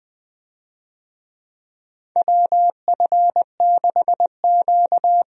Morse code, heard as a keyed tone, about 20 wpm, 700 Hz. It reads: WF6Q
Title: USA Call Signs - 4 Characters - Repeated 3x - 20wpm